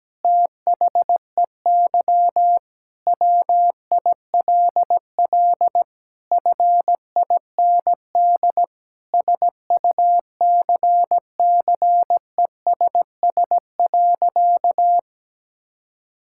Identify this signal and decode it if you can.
Morse code, heard as a keyed tone, about 17 wpm, 700 Hz. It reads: THEY WILL FIND SUCCESS.
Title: Sentences from Top 1000 Words - Review Part 6 - 17wpm